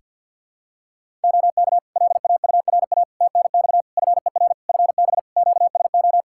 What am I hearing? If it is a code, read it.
Morse code, heard as a keyed tone, about 46 wpm, 700 Hz. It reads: OK LAURA TNX FER FB QSO
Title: K1USN SST QSO Practice 50wpm